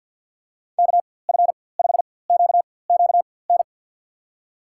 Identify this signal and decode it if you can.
Morse code, heard as a keyed tone, about 53 wpm, spaced as for 24 wpm, 700 Hz. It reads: KF5QQN
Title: USA Call Signs - 6 Characters - 4x Character Spacing - 50wpm